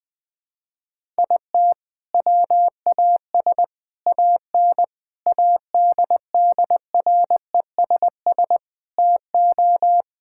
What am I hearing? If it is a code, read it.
Morse code, heard as a keyed tone, about 20 wpm, 700 Hz. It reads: IT WAS AN ADDRESS TO
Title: Sentences from Top 1000 Words - Review Part 6 - 20wpm